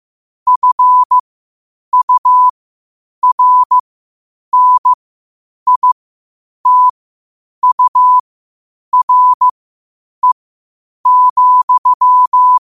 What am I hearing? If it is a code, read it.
Morse code, heard as a keyed tone, about 15 wpm, spaced as for 8 wpm, 1000 Hz. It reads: FURNITURE,
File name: qotdfiles / 8wpm